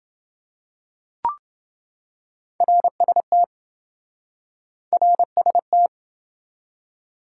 Morse code, heard as a keyed tone, about 28 wpm, 700 Hz. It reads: RST RST